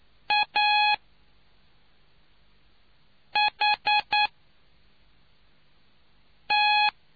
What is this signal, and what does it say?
Morse code, heard as a keyed tone, about 9 wpm, 840 Hz. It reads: A H T